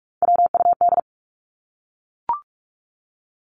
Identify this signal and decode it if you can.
Morse code, heard as a keyed tone, about 44 wpm, 700 Hz. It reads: WUD
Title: Abbreviations 45wpm